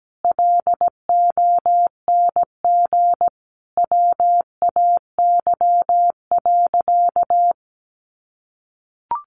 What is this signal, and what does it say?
Morse code, heard as a keyed tone, about 17 wpm, 700 Hz. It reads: LONG WAY.
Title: Sentences from Top 600 Words 17wpm